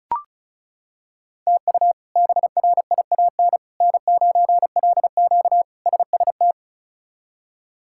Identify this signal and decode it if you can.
Morse code, heard as a keyed tone, about 36 wpm, 700 Hz. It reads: TU BRIAN N9LQ SST
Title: K1USN SST QSO Practice - 35wpm